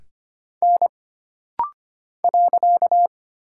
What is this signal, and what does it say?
Morse code, heard as a keyed tone, about 25 wpm, 700 Hz. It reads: N .
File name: Single Character (Letter:Number:Punctuation:Prosign) - Rapid-Fire - 25wpm